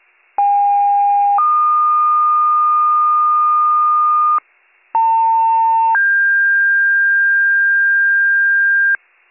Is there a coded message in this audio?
Yes: two-tone page from 0.4 to 9.0 s, heard as a one-second tone, then a three-second tone, about 810 then 1220 Hz, about 900 then 1600 Hz, with noise.